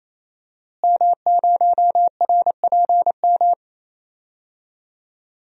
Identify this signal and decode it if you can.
Morse code, heard as a keyed tone, about 28 wpm, 700 Hz. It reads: M0RPM
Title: DXCC-100 Call Signs - 5 Characters - Repeated 3x - 28wpm